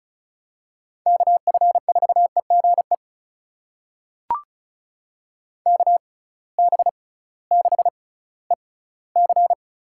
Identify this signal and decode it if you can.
Morse code, heard as a keyed tone, about 36 wpm, spaced as for 19 wpm, 700 Hz. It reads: KF4EGE KB6EC
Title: USA Call Signs - 6 Characters - 6x Character Spacing - 35wpm